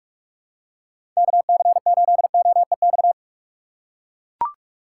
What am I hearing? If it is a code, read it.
Morse code, heard as a keyed tone, about 43 wpm, 700 Hz. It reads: KC8OEX
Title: USA Call Signs - 6 Characters - 3x Character Spacing - 45wpm